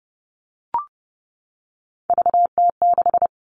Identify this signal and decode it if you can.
Morse code, heard as a keyed tone, about 30 wpm, 700 Hz. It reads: VT6
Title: All Cut Numbers - 3-Digit 30wpm